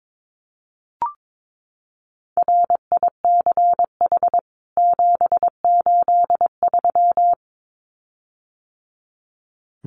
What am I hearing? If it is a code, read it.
Morse code, heard as a keyed tone, about 22 wpm, 700 Hz. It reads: RICH 783
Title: CWT Exchange 22wpm